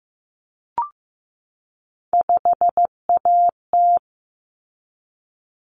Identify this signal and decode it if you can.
Morse code, heard as a keyed tone, about 15 wpm, 700 Hz. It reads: 5AT
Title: Common Cut Numbers - 3-Digit 15wpm